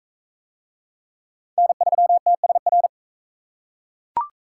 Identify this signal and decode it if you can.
Morse code, heard as a keyed tone, about 41 wpm, 700 Hz. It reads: N3TSR